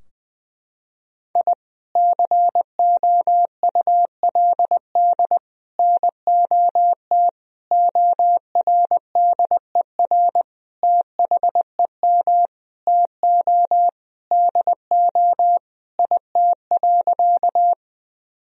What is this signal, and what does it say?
Morse code, heard as a keyed tone, about 21 wpm, 700 Hz. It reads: I COULD NOT ORDER THEM TO DO IT.
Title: Sentences from Top 1000 Words - Review Part 2 - 20wpm